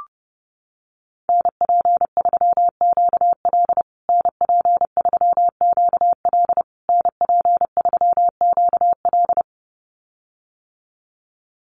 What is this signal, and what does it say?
Morse code, heard as a keyed tone, about 30 wpm, 700 Hz. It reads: NP3QL NP3QL NP3QL
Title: USA Call Signs Repeated 3x 30wpm